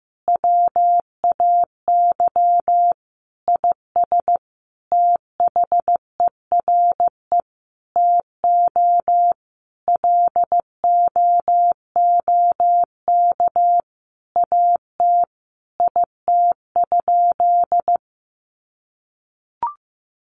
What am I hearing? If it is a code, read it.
Morse code, heard as a keyed tone, about 15 wpm, 700 Hz. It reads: WAY IS THERE TO LOOK AT IT?
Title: Sentences from Top 100 Words 15wpm